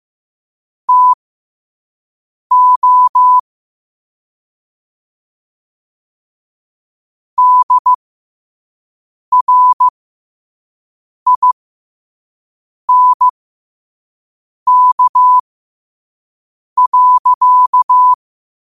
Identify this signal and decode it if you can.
Morse code, heard as a keyed tone, about 15 wpm, spaced as for 5 wpm, 1000 Hz. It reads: TO DRINK.
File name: qotdfiles / 5wpm